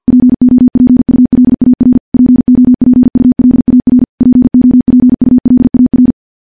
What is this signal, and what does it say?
Morse code, heard as a keyed tone, about 51 wpm, 250 Hz. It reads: POPULAR POPULAR POPULAR
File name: morse / morse-training-50WPM